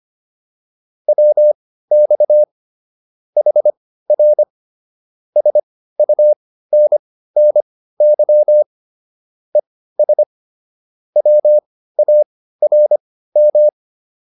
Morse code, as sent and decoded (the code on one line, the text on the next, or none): .-- -..- / .... .-. / ... ..- -. -. -.-- / . ... / .-- .- .-. --
WX HR SUNNY ES WARM